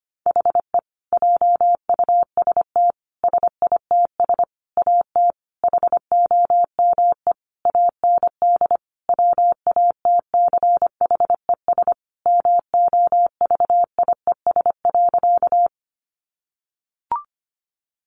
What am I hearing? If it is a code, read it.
Morse code, heard as a keyed tone, about 25 wpm, 700 Hz. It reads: HE JUST SITS AT HOME AND WATCHES MOVIES.
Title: Additional Easy Phrases 25wpm